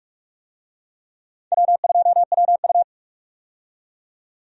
Morse code, heard as a keyed tone, about 41 wpm, 700 Hz. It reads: W2WU
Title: USA Call Signs - 4 Characters - 45wpm